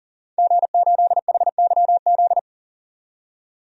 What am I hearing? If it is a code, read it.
Morse code, heard as a keyed tone, about 40 wpm, 700 Hz. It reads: G8HYZ
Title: DXCC-100 Call Signs - 5 Characters - 40wpm